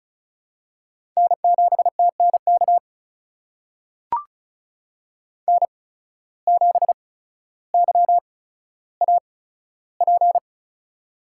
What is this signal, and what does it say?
Morse code, heard as a keyed tone, about 36 wpm, spaced as for 15 wpm, 700 Hz. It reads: N7TNK N7YAP
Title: USA Call Signs - 5 Characters - 8x Character Spacing - 35wpm